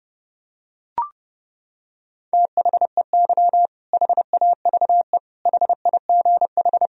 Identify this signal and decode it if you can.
Morse code, heard as a keyed tone, about 30 wpm, 700 Hz. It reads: THEY HAVE HIGH
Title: Sentences from Top 1000 Words - Review Part 4 - 30wpm